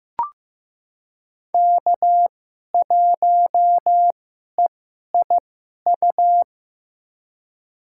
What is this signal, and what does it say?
Morse code, heard as a keyed tone, about 15 wpm, spaced as for 11 wpm, 700 Hz. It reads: K1EIU
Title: USA Call Signs - 5 Characters - 2x Character Spacing - 15wpm